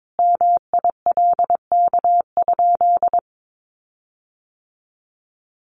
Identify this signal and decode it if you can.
Morse code, heard as a keyed tone, about 22 wpm, 700 Hz. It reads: MILK?
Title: Sentences from Top 1500 Words 22wpm